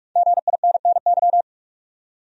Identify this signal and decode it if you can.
Morse code, heard as a keyed tone, about 46 wpm, 700 Hz. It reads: GINNY
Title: USA Common Names - 45wpm